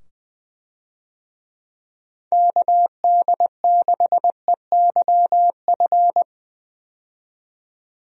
Morse code, heard as a keyed tone, about 20 wpm, 700 Hz. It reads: KD6EYF
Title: USA Call Signs - 6 Characters - 20wpm